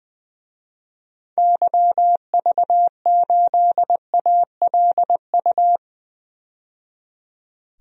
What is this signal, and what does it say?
Morse code, heard as a keyed tone, about 20 wpm, 700 Hz. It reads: YV8ALU
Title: DXCC-100 Call Signs - 6 Characters - 20wpm